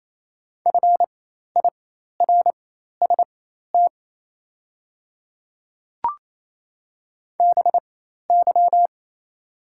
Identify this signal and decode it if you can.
Morse code, heard as a keyed tone, about 28 wpm, spaced as for 13 wpm, 690 Hz. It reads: FIRST BY
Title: 4x Character Spacing - Top 500 Words - 28wpm